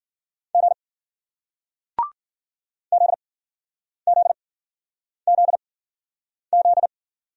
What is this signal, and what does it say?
Morse code, heard as a keyed tone, about 54 wpm, spaced as for 9 wpm, 700 Hz. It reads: G ZZZZ